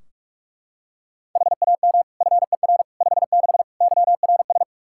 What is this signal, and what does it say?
Morse code, heard as a keyed tone, about 41 wpm, 700 Hz. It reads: HAM FER 56 YRS
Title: Practice QSO - Exchange 3 - More Info - 45wpm